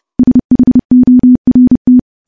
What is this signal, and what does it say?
Morse code, heard as a keyed tone, about 30 wpm, 250 Hz. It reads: SHORT
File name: morse-training-30WPM